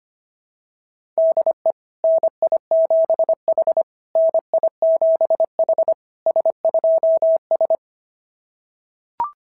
Morse code, heard as a keyed tone, about 25 wpm, 690 Hz. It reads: DE NI7H NI7H S2S
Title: SOTA QSO Practice 25wpm